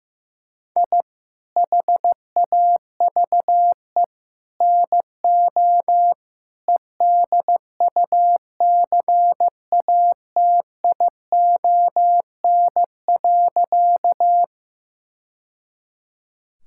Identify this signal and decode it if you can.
Morse code, heard as a keyed tone, about 16 wpm, 700 Hz. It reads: I HAVE NO EDUCATION.
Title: Sentences from Top 1000 Words - Review Part 6 - 15wpm